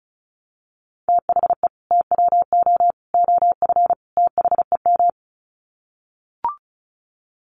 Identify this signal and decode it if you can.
Morse code, heard as a keyed tone, about 34 wpm, 700 Hz. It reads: THE TWO OF THEM
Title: Sets of 4 Words 35wpm